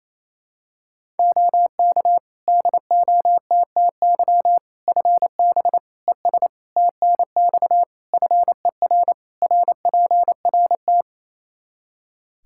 Morse code, heard as a keyed tone, about 28 wpm, 700 Hz. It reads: OK DOTTY FB ES TNX FER RPRT
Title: K1USN SST QSO Practice 28wpm